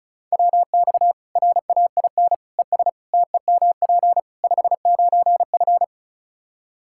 Morse code, heard as a keyed tone, about 36 wpm, 700 Hz. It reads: WX RAIN ES TEMP 59F
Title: Practice QSO - Exchange 2 - Info - 35wpm